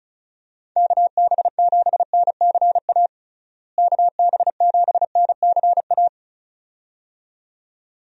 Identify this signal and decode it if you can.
Morse code, heard as a keyed tone, about 34 wpm, 700 Hz. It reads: KB7NCA KB7NCA
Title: USA Call Signs - 6 Characters - Repeated 2x - 35wpm